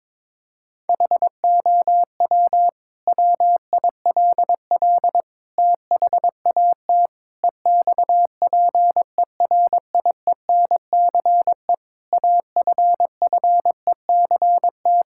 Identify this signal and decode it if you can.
Morse code, heard as a keyed tone, about 22 wpm, 700 Hz. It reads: HOW WILL THAT EXPERIENCE AFFECT
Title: Sentences from Top 1000 Words - Review Part 6 - 22wpm